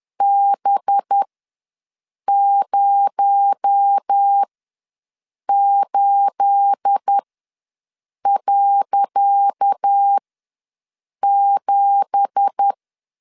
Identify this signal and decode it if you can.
Morse code, heard as a keyed tone, about 10 wpm, spaced as for 6 wpm, 800 Hz. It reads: B08.7
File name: LEZIONI / 10 SIGNS / 10C DOT ws 10wpm 4213